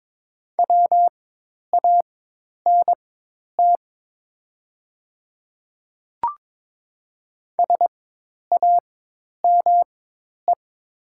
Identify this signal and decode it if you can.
Morse code, heard as a keyed tone, about 22 wpm, spaced as for 10 wpm, 700 Hz. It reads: WANT SAME
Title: 4x Character Spacing - Top 200 Words - 22wpm